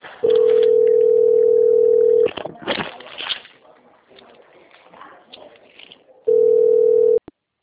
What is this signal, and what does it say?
Ringback tone (North America), 440 + 480 Hz.
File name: out- - - - - .15352